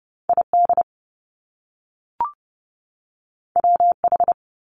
Morse code, heard as a keyed tone, about 30 wpm, 700 Hz. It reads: ID WH
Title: Top Two Letter Combinations in Words 30wpm